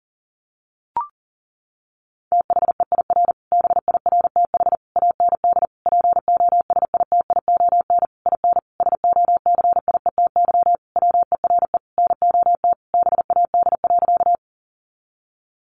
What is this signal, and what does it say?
Morse code, heard as a keyed tone, about 40 wpm, 700 Hz. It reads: THEIR BIRTH AND POSITION IN SOCIETY WERE NOT BAD.
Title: Sentences from Top 900 Words 40wpm